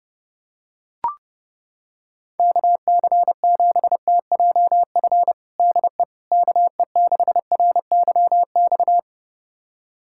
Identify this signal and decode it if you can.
Morse code, heard as a keyed tone, about 32 wpm, 700 Hz. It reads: KC7TJF DE KE6RYX
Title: USA Callsign DE Callsign - 6 Characters - 30wpm